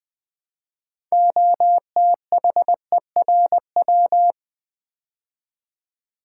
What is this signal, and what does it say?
Morse code, heard as a keyed tone, about 20 wpm, 700 Hz. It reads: OTHERW
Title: Build A Word - 3 Letters At A Time - 9-Letter Words - 20wpm